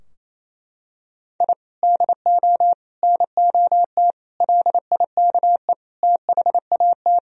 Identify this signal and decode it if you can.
Morse code, heard as a keyed tone, about 28 wpm, 700 Hz. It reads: I DO NOT LIKE THAT
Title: Sentences from Top 1000 Words - Review Part 2 - 28wpm